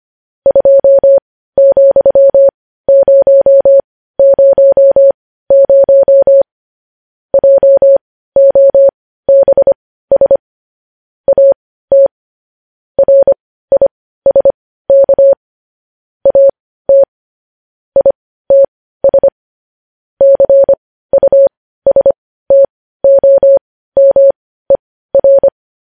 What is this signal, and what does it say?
Morse code, heard as a keyed tone, about 25 wpm, spaced as for 15 wpm, 550 Hz. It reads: 2,000 JOBS AT RISK AT ITS CUSTOMER
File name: podcast / News headlines-1x-25x15